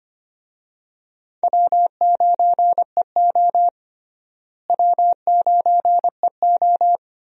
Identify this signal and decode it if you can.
Morse code, heard as a keyed tone, about 25 wpm, 700 Hz. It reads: W9EO W9EO
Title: USA Call Signs - 4 Characters - Repeated 3x - 25wpm